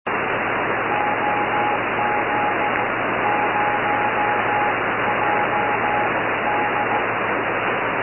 Morse code, heard as a keyed tone, about 15 wpm, spaced as for 13 wpm, 830 Hz, with noise. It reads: OZ0JD